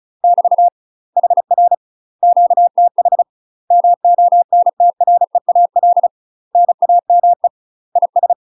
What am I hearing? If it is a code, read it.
Morse code, heard as a keyed tone, about 36 wpm, 700 Hz. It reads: <BT> HR QTH MONTREAL NAME IS
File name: UK MASTER QSOs TRAINING / 35WPM / UK MASTER QSOs 0005